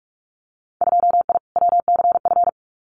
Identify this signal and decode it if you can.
Morse code, heard as a keyed tone, about 46 wpm, 700 Hz. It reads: 2I WCF
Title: Winter Field Day Exchanges 45wpm